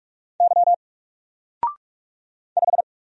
Morse code, heard as a keyed tone, about 46 wpm, 700 Hz. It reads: Y 5
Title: Single Letter-Number Alt 45wpm